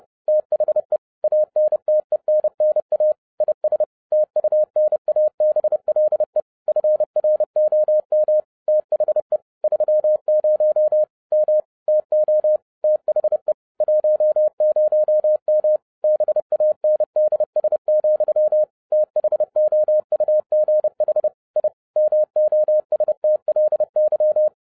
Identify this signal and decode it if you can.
Morse code, heard as a keyed tone, about 30 wpm, 610 Hz. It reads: THE ANTENNA IS TUNABLE FROM THE 30 M TO THE 10M BANDS, THOUGH I MOSTLY